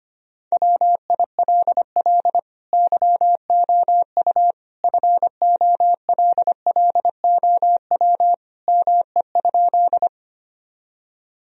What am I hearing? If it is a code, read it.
Morse code, heard as a keyed tone, about 25 wpm, 700 Hz. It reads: WILL YOU FOLLOW ME?
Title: Sentences from Top 1000 Words - Review Part 1 - 25wpm